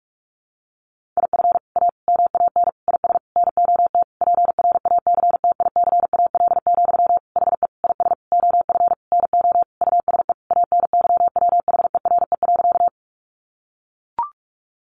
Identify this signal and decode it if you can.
Morse code, heard as a keyed tone, about 44 wpm, 700 Hz. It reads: IF A MAN IS NOT PRACTICAL, HE IS OF NO USE ANYWHERE.
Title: Sentences from Top 1400 Words 45wpm